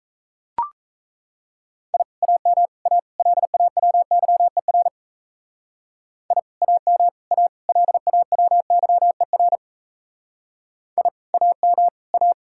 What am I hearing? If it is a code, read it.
Morse code, heard as a keyed tone, about 36 wpm, 700 Hz. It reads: I AM A LAWYER I AM A LAWYER I AM A